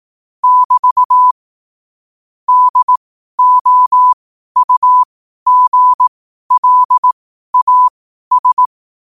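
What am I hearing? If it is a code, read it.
Morse code, heard as a keyed tone, about 18 wpm, spaced as for 12 wpm, 1000 Hz. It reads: <BT> DOUGLAS
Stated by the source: Morse code sent at 12 words per minute.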